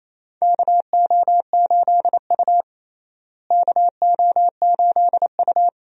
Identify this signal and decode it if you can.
Morse code, heard as a keyed tone, about 28 wpm, 700 Hz. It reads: KO8U KO8U
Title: USA Call Signs - 4 Characters - Repeated 3x - 28wpm